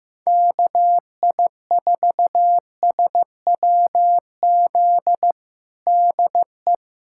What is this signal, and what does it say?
Morse code, heard as a keyed tone, about 15 wpm, 700 Hz. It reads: KI4SWZ DE
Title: USA Callsign DE Callsign 15wpm